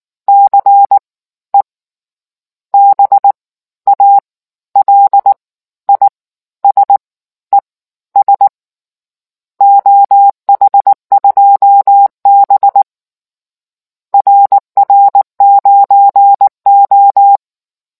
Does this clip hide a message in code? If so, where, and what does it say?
Yes: Morse code from 0.3 to 17.4 s, heard as a keyed tone, about 19 wpm, spaced as for 13 wpm, 800 Hz. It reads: CE BALISES OH2B RR9O